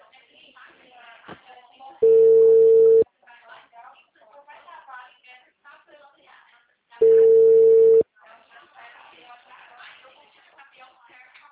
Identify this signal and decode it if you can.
Ringback tone (China, 450 Hz), with speech.